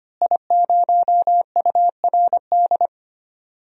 Morse code, heard as a keyed tone, about 25 wpm, 700 Hz. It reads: I0URD
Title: DXCC-100 Call Signs - 5 Characters - Repeated 3x - 25wpm